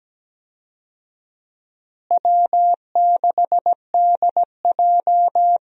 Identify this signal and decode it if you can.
Morse code, heard as a keyed tone, about 17 wpm, 700 Hz. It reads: W6DJ
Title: USA Call Signs - 4 Characters - Repeated 3x - 17wpm